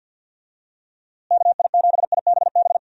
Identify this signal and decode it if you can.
Morse code, heard as a keyed tone, about 46 wpm, 700 Hz. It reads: KI7IBB
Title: USA Call Signs - 6 Characters - Repeated 2x - 50wpm